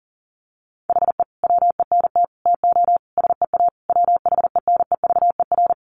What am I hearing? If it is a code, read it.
Morse code, heard as a keyed tone, about 40 wpm, 700 Hz. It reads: HE WENT TO SEA WHENEVER
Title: Sentences from Top 1200 Words 40wpm